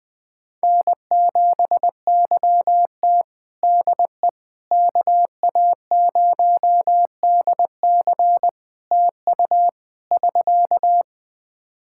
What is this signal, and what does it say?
Morse code, heard as a keyed tone, about 21 wpm, 700 Hz. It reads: N7YT DE KA0DC TU <SK>
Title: Practice QSO - Exchange 4 - Ending - 20wpm